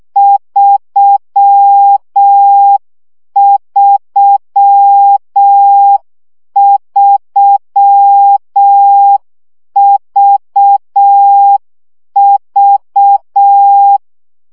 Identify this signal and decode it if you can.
Morse code, heard as a keyed tone, about 6 wpm, 800 Hz. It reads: 333VV
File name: CW gyakorlas v 3 5 baud